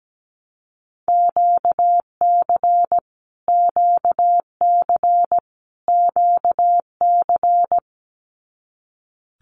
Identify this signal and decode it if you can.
Morse code, heard as a keyed tone, about 17 wpm, 700 Hz. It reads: QC QC QC
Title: Canadian Provinces and Territories - Repeated 3x - 17wpm